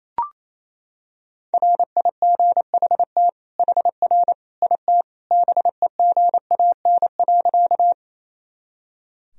Morse code, heard as a keyed tone, about 28 wpm, 700 Hz. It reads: RIGHT HR IT BEGAN.
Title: Sentences from Top 300 Words and Abbreviations 28wpm